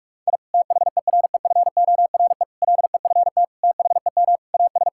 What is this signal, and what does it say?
Morse code, heard as a keyed tone, about 46 wpm, 700 Hz. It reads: I THEREFORE LEFT THEM AS
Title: Sentences from Top 1000 Words - Review Part 5 - 45wpm